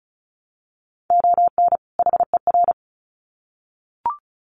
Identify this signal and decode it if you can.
Morse code, heard as a keyed tone, about 34 wpm, 700 Hz. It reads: ON HER
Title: Sets of 2 Words 35wpm